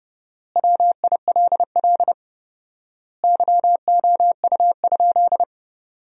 Morse code, heard as a keyed tone, about 30 wpm, 700 Hz. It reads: WILL YOU?
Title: 4x Word-Spacing - Sentences from Top 400 Words - 30wpm